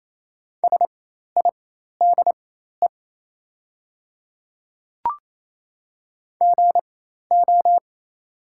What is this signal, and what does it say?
Morse code, heard as a keyed tone, about 27 wpm, spaced as for 13 wpm, 700 Hz. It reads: SIDE GO
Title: 4x Character Spacing - Top 100 Words - 28wpm